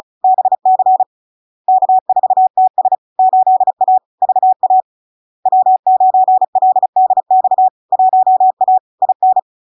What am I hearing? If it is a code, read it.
Morse code, heard as a keyed tone, about 36 wpm, 750 Hz. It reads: BC K4TS 8A VA W9LDX 1A IN
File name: LARK-FD-2022-Log-in-CW-35wpm-750-Hz